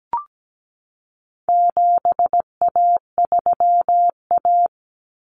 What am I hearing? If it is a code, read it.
Morse code, heard as a keyed tone, about 17 wpm, 700 Hz. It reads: 7A3A
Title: Common Cut Numbers - 4-Digit 17wpm